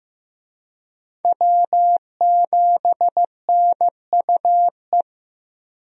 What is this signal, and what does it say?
Morse code, heard as a keyed tone, about 15 wpm, 700 Hz. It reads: W7NUE
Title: USA Call Signs - 5 Characters - Repeated 2x - 15wpm